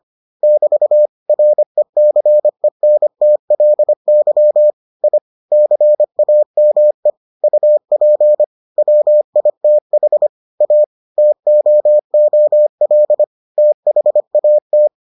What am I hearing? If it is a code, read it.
Morse code, heard as a keyed tone, about 25 wpm, 600 Hz. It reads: <BT> RECENTLY I CAME UP WITH A TOOL THAT